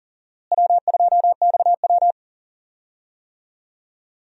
Morse code, heard as a keyed tone, about 39 wpm, 700 Hz. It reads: W2XW
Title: USA Call Signs - 4 Characters - Repeated 3x - 40wpm